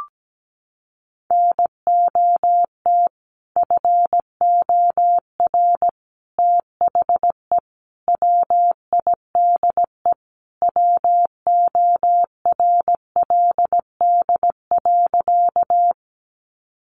Morse code, heard as a keyed tone, about 17 wpm, 700 Hz. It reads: NOT FOR THE WIDE WORLD.
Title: Sentences from Top 900 Words 17wpm